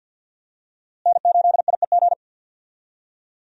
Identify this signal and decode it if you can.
Morse code, heard as a keyed tone, about 47 wpm, 700 Hz. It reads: N8IEG
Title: USA Call Signs - 5 Characters - Repeated 2x - 50wpm